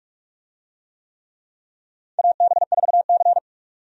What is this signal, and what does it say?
Morse code, heard as a keyed tone, about 46 wpm, 700 Hz. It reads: AB4C
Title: USA Call Signs - 4 Characters - 45wpm